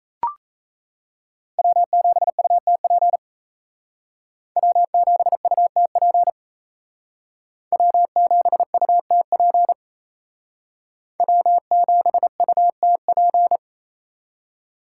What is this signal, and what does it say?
Morse code, heard as a keyed tone, about 34 wpm, 700 Hz. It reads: W7UTP W7UTP W7UTP W7UTP
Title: Speed Racing - USA 1x3 Call Signs 28wpm